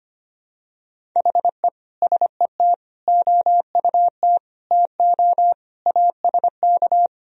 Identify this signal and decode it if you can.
Morse code, heard as a keyed tone, about 25 wpm, 700 Hz. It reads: HE SET OUT TO ASK
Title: Sentences from Top 1000 Words - Review Part 1 - 25wpm